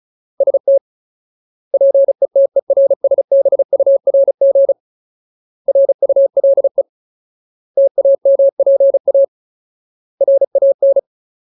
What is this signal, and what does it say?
Morse code, heard as a keyed tone, about 34 wpm, 550 Hz. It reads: ST PETERSBURG RULE TAMPA RAN
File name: Ultimate Word List / 35WPM Ultimate